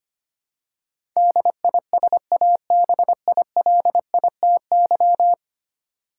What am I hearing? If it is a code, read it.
Morse code, heard as a keyed tone, about 25 wpm, 700 Hz. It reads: DISABILITY
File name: Ten Letter Words - 25wpm